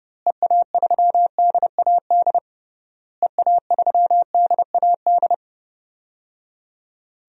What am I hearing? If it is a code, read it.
Morse code, heard as a keyed tone, about 30 wpm, 700 Hz. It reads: EA3DAD EA3DAD
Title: DXCC-100 Call Signs - 6 Characters - Repeated 3x - 30wpm